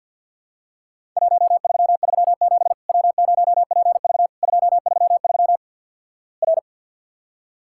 Morse code, heard as a keyed tone, about 50 wpm, 700 Hz. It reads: 1337 W0PV 233 R
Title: Worked All Europe DX Contest - QTC Exchange - 50wpm